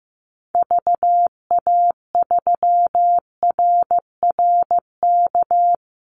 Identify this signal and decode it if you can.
Morse code, heard as a keyed tone, about 15 wpm, 700 Hz. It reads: VA3RRK
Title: Call Signs Repeated 3x - 15wpm